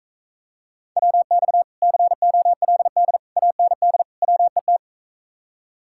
Morse code, heard as a keyed tone, about 45 wpm, 700 Hz. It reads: WX COLD AND WET